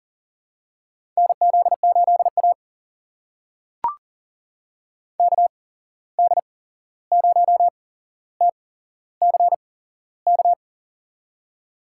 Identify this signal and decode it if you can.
Morse code, heard as a keyed tone, about 40 wpm, spaced as for 15 wpm, 700 Hz. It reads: NZ8A KD0TCK
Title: USA Call Signs - 8x Character Spacing - 40wpm